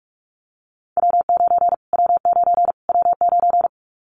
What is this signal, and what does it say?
Morse code, heard as a keyed tone, about 44 wpm, 700 Hz. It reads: W9 W9 W9
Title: Callsign Prefixes for Top 25 DXCC - Repeated 3x - 45wpm